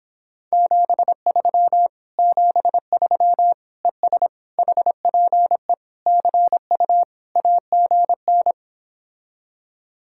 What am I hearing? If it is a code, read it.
Morse code, heard as a keyed tone, about 29 wpm, 700 Hz. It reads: 73 73 ES HPE CU AGN